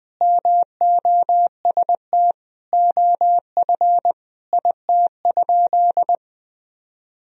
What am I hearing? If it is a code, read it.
Morse code, heard as a keyed tone, about 21 wpm, 700 Hz. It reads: MOST OF IT?
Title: Sentences from Top 1000 Words - Review Part 1 - 20wpm